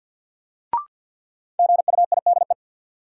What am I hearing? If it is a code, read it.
Morse code, heard as a keyed tone, about 46 wpm, 700 Hz. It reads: GUIDE
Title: Sentences from Top 600 Words - New Word Drill - 50wpm